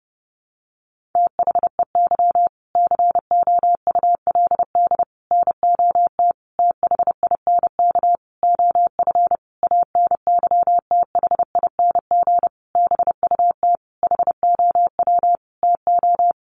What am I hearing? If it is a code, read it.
Morse code, heard as a keyed tone, about 30 wpm, 700 Hz. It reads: THEY COULD NOT THINK OF ANYTHING BUT HOW TO
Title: Sentences from Top 1000 Words 30wpm